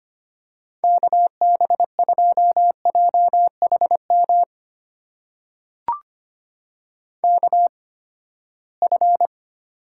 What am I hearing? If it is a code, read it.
Morse code, heard as a keyed tone, about 25 wpm, spaced as for 17 wpm, 700 Hz. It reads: KB2JHM KF
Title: USA Call Signs - 6 Characters - 8x Character Spacing - 25wpm